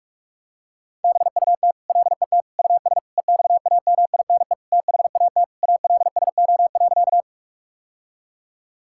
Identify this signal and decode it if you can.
Morse code, heard as a keyed tone, about 46 wpm, 700 Hz. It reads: BUT LET US EXAMINE THAT ALSO.
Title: Sentences from Top 1600 Words 45wpm